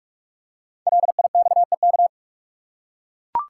Morse code, heard as a keyed tone, about 44 wpm, 700 Hz. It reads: LIXEK
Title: Code Groups - 5 Letter:Numbers - 45wpm